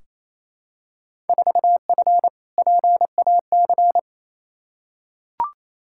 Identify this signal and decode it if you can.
Morse code, heard as a keyed tone, about 28 wpm, 700 Hz. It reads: 4F PAC
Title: Field Day Exchanges Encore - 28wpm